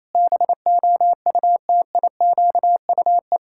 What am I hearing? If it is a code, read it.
Morse code, heard as a keyed tone, about 28 wpm, 700 Hz. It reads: BOUTIQUE
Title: Pangram Challenge v2 - 28wpm